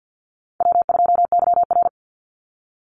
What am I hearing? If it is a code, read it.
Morse code, heard as a keyed tone, about 50 wpm, 700 Hz. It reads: W2YR
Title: Call Signs - 50wpm